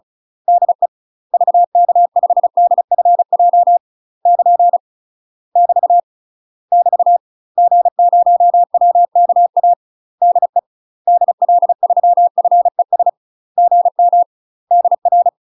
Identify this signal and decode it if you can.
Morse code, heard as a keyed tone, about 34 wpm, 700 Hz. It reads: DE VK5DFJ <KN> <BT> <BT> G0WKA DE DL3FES GM DR
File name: UK MASTER QSOs TRAINING / 35WPM / UK MASTER QSOs 0015